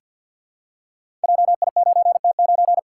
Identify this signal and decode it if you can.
Morse code, heard as a keyed tone, about 53 wpm, 700 Hz. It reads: JI9T9
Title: Code Groups - 5 Letter:Numbers - 50wpm